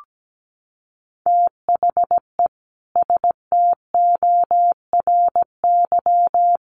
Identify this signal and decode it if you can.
Morse code, heard as a keyed tone, about 17 wpm, 700 Hz. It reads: THE STORY
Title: Sentences from Top 1200 Words 17wpm